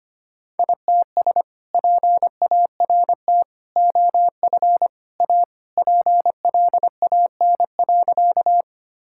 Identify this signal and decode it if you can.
Morse code, heard as a keyed tone, about 25 wpm, 700 Hz. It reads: ITS PART OF A PLAN.
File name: practice / 25wpm / Sentences with 3-Word Phrases - 25wpm